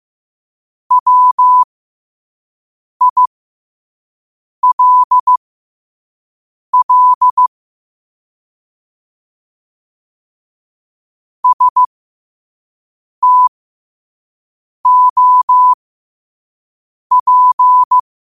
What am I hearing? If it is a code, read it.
Morse code, heard as a keyed tone, about 15 wpm, spaced as for 5 wpm, 1000 Hz. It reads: WILL STOP
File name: qotdfiles / 5wpm